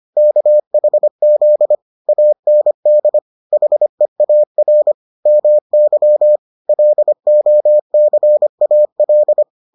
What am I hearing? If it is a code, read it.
Morse code, heard as a keyed tone, about 25 wpm, 600 Hz. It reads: KHZ AND HEAR MY LOCAL